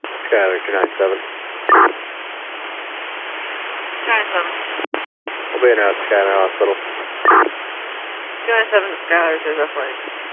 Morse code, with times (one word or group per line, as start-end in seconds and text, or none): none